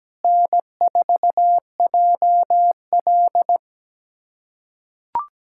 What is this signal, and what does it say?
Morse code, heard as a keyed tone, about 17 wpm, 700 Hz. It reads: N4JL
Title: USA Callsign DE Callsign - 4 Characters - 17wpm